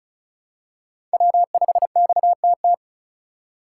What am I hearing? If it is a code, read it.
Morse code, heard as a keyed tone, about 34 wpm, 700 Hz. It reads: W5XTT
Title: USA Call Signs - 5 Characters - 4x Character Spacing - 35wpm